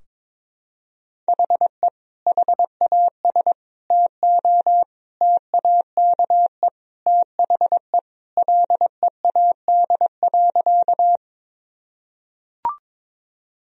Morse code, heard as a keyed tone, about 22 wpm, 700 Hz. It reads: HE HAS TO TAKE THE LEAD.
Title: Sentences from Top 1000 Words - Review Part 2 - 22wpm